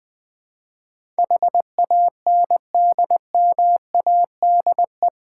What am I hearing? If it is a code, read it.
Morse code, heard as a keyed tone, about 21 wpm, 700 Hz. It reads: HANDMADE
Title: Difficult Words with Hints - 20wpm